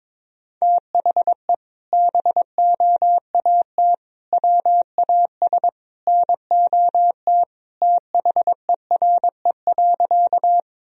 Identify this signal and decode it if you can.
Morse code, heard as a keyed tone, about 22 wpm, 700 Hz. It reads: THE BOAT WAS NOT THERE.
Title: Sentences from Top 1000 Words - Review Part 3 - 22wpm